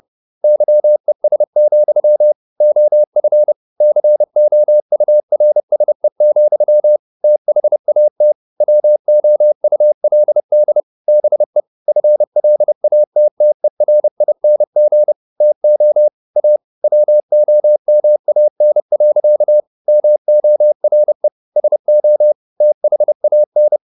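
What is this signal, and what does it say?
Morse code, heard as a keyed tone, about 30 wpm, 600 Hz. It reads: YES, OF COURSE, THAT WOULD BE FLATTERING TO A WOMAN. MORE SO THAN
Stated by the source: Morse code at 30 wpm, the novel Never by Ken Follett in English, from CW med Gnister.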